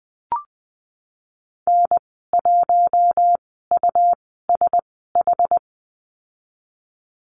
Morse code, heard as a keyed tone, about 20 wpm, spaced as for 15 wpm, 700 Hz. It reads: N1USH
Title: 245 Lesson 39g - Letter z - Call Signs 20wpm